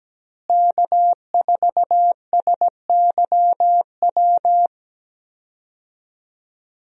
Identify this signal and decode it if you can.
Morse code, heard as a keyed tone, about 17 wpm, 700 Hz. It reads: K4SYW
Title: USA Call Signs - 5 Characters - Repeated 3x - 17wpm